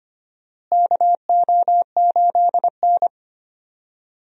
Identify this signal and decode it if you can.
Morse code, heard as a keyed tone, about 25 wpm, 700 Hz. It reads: KO8N
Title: USA Call Signs - 4 Characters - Repeated 3x - 25wpm